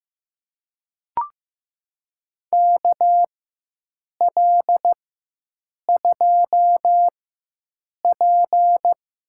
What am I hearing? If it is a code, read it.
Morse code, heard as a keyed tone, about 15 wpm, spaced as for 7 wpm, 700 Hz. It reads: KL2P
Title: USA Call Signs - 4x Character Spacing - 15wpm